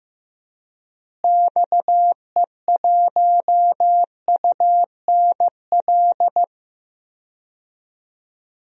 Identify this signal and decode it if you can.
Morse code, heard as a keyed tone, about 15 wpm, 700 Hz. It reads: XE1UNL
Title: DXCC-100 Call Signs - 6 Characters - Repeated 3x - 15wpm